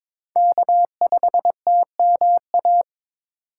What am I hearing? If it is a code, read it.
Morse code, heard as a keyed tone, about 22 wpm, 700 Hz. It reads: K5TMA
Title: USA Call Signs - 2x Character Spacing - 22wpm